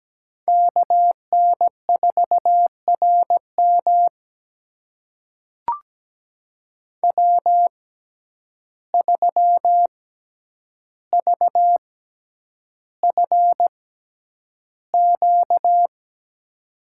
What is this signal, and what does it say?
Morse code, heard as a keyed tone, about 17 wpm, spaced as for 9 wpm, 700 Hz. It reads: KN4RM W3VFQ
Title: USA Call Signs - 5 Characters - 6x Character Spacing - 17wpm